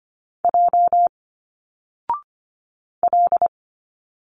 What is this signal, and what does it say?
Morse code, heard as a keyed tone, about 25 wpm, 700 Hz. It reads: J L
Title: Single Letters 25wpm